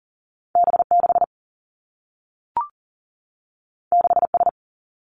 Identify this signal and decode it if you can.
Morse code, heard as a keyed tone, about 40 wpm, 700 Hz. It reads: B 6 6 S
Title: Two Challenging Letter-Number Combinations 40wpm